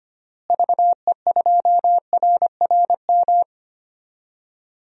Morse code, heard as a keyed tone, about 25 wpm, 700 Hz. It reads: VE2RRM
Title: DXCC-100 Call Signs - 6 Characters - Repeated 2x - 25wpm